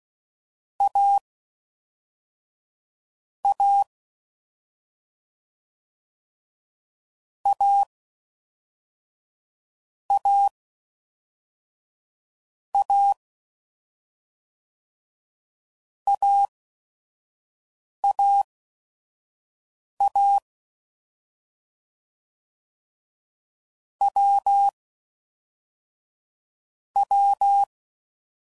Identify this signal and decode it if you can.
Morse code, heard as a keyed tone, about 16 wpm, 800 Hz. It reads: A A A A A A A A W W
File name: Combinations / Difficult Combos / AWJ